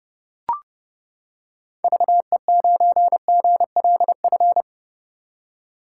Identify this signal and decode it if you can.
Morse code, heard as a keyed tone, about 31 wpm, 700 Hz. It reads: VE9GLF
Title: K1USN SST QSO Practice - 30wpm